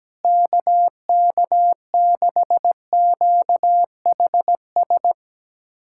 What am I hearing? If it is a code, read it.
Morse code, heard as a keyed tone, about 17 wpm, 700 Hz. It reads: KK6QHS
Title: USA Callsign DE Callsign 17wpm